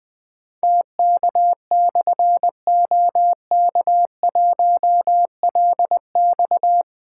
Morse code, heard as a keyed tone, about 20 wpm, 700 Hz. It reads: TK/OK1LX
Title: Difficult Call Signs 20wpm